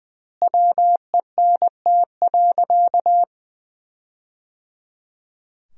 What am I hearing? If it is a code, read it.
Morse code, heard as a keyed tone, about 20 wpm, 700 Hz. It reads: WENT.
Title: 2x Word-Spacing - Sentences from Top 200 Words - 20wpm